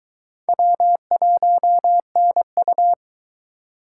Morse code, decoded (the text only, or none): W1NU